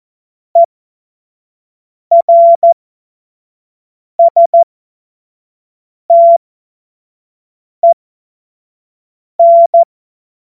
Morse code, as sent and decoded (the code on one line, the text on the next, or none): . .-. ... - . -.
ERSTEN